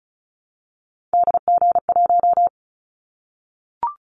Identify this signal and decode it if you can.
Morse code, heard as a keyed tone, about 34 wpm, 700 Hz. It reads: DG1
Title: Callsign Prefixes for Top 25 DXCC - 35wpm